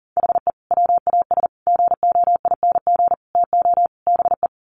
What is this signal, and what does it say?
Morse code, heard as a keyed tone, about 40 wpm, 700 Hz. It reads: HE WAS GOING TO BE
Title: Sets of 5 Words 40wpm